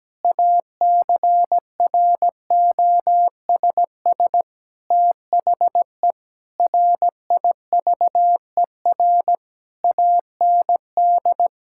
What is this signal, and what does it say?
Morse code, heard as a keyed tone, about 17 wpm, 700 Hz. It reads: ACROSS THE RIVER AND
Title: Sentences from Top 1000 Words - Review Part 6 - 17wpm